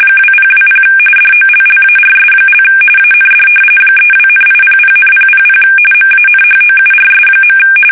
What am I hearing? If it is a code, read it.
RTTY, about 45.45 baud. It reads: 2020 DOLLARS) AND 32,800,000 MANHOURS.
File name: fsk850 / battleshipnj